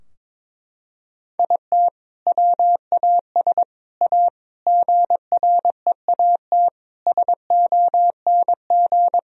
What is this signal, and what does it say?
Morse code, heard as a keyed tone, about 21 wpm, 700 Hz. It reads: IT WAS A GREAT SONG
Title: Sentences from Top 1000 Words - Review Part 4 - 22wpm